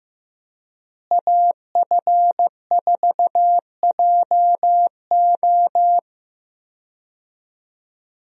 Morse code, heard as a keyed tone, about 15 wpm, 700 Hz. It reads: AF4JO
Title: USA Call Signs - 5 Characters - Repeated 3x - 15wpm